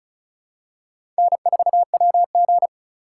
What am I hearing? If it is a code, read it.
Morse code, heard as a keyed tone, about 34 wpm, 700 Hz. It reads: N4WG
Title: USA Call Signs - 4x Character Spacing - 35wpm